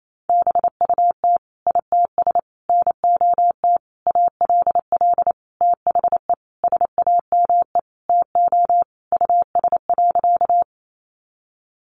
Morse code, heard as a keyed tone, about 28 wpm, 700 Hz. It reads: BUT ITS NOT ALL THE SAME TO US.
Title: Sentences from Top 600 Words 28wpm